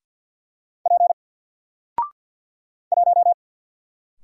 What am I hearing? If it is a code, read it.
Morse code, heard as a keyed tone, about 50 wpm, 700 Hz. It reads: P 1
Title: Single Letter-Number 50wpm